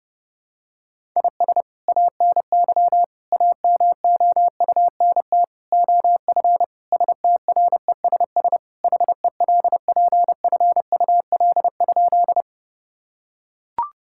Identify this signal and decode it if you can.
Morse code, heard as a keyed tone, about 30 wpm, 700 Hz. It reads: IS ANY AMOUNT OF STRESS HELPFUL?
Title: Sentences from Top 1000 Words - Review Part 6 - 30wpm